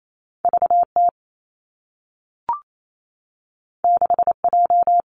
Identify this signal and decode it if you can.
Morse code, heard as a keyed tone, about 28 wpm, 700 Hz. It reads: V T 6 J
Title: Two Challenging Letter-Number Combinations 28wpm